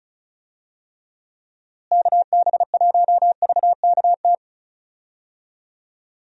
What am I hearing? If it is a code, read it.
Morse code, heard as a keyed tone, about 34 wpm, 700 Hz. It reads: KB1VKT
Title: USA Call Signs - 6 Characters - Repeated 3x - 35wpm